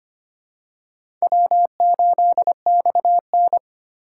Morse code, heard as a keyed tone, about 25 wpm, 700 Hz. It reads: W8XN
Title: USA Call Signs - 4 Characters - 3x Character Spacing - 25wpm